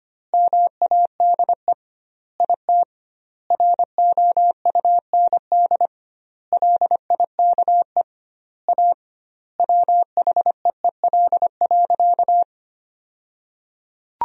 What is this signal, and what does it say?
Morse code, heard as a keyed tone, about 25 wpm, 700 Hz. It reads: MADE IT ROUND LIKE A WHEEL.
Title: 2x Word-Spacing - Sentences from Top 500 Words - 25wpm